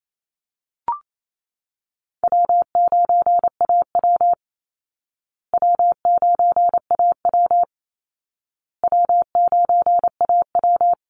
Morse code, heard as a keyed tone, about 28 wpm, 700 Hz. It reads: W9AW W9AW W9AW
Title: Easy Call Signs Repeated 3x - 28wpm